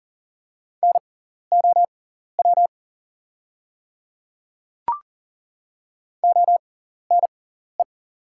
Morse code, heard as a keyed tone, about 38 wpm, spaced as for 14 wpm, 690 Hz. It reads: NOW ONE
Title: 6x Character Spacing - Top 100 Words - 40wpm